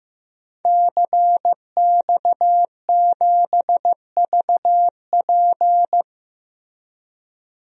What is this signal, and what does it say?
Morse code, heard as a keyed tone, about 15 wpm, 700 Hz. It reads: CX7VP
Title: DXCC-100 Call Signs - 5 Characters - Repeated 2x - 15wpm